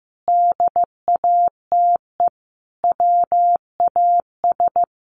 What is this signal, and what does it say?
Morse code, heard as a keyed tone, about 15 wpm, 700 Hz. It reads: DATE WAS
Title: Sentences from Top 1100 Words 15wpm